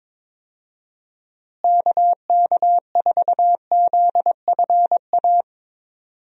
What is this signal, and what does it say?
Morse code, heard as a keyed tone, about 21 wpm, 700 Hz. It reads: KK4ZFA
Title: USA Call Signs - 6 Characters - Repeated 3x - 22wpm